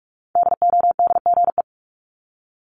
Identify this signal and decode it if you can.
Morse code, heard as a keyed tone, about 46 wpm, 700 Hz. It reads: DODGE
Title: Multi-letter Phonemes 45wpm